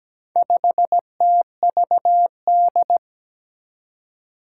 Morse code, heard as a keyed tone, about 17 wpm, 700 Hz. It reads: 5TVD
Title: Four Challenging Letter-Number Combinations 17wpm